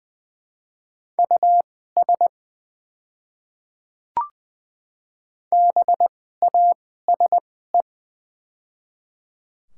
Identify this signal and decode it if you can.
Morse code, heard as a keyed tone, about 20 wpm, spaced as for 14 wpm, 700 Hz. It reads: US BASE
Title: 2x Character Spacing - Top 300 Words - 20wpm